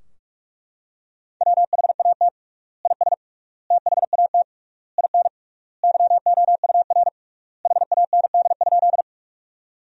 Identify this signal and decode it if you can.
Morse code, heard as a keyed tone, about 42 wpm, 700 Hz. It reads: WHAT IS THAT IN YOUR HAND?
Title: 3x Word-Spacing - Sentences from Top 200 Words - 45wpm